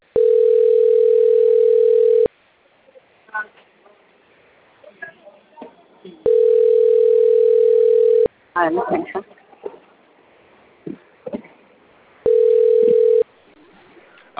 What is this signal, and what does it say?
Ringback tone (North America), 440 + 480 Hz, with speech.
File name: out- - - - - .286